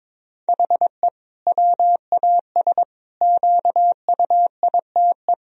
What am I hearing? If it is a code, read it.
Morse code, heard as a keyed tone, about 22 wpm, 700 Hz. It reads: HE WAS QUITE
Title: Sentences from Top 1000 Words - Review Part 3 - 22wpm